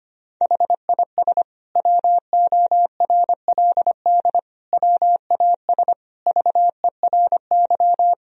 Morse code, heard as a keyed tone, about 25 wpm, 700 Hz. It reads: HIS WORLD WAS VERY
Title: Sentences from Top 1000 Words - Review Part 1 - 25wpm